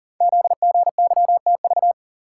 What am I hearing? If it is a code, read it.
Morse code, heard as a keyed tone, about 40 wpm, 700 Hz. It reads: ZGYTV
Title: Code Groups - 5 Letters - 40wpm